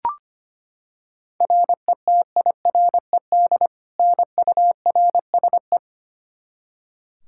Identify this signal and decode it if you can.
Morse code, heard as a keyed tone, about 25 wpm, 700 Hz. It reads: RETIRED NURSE
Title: Practice QSO - Exchange 3 - More Info - 25wpm